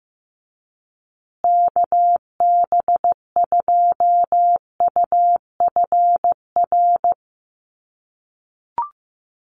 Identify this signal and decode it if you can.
Morse code, heard as a keyed tone, about 15 wpm, 700 Hz. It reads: KB2UFR
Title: USA Call Signs Repeated 3x 15wpm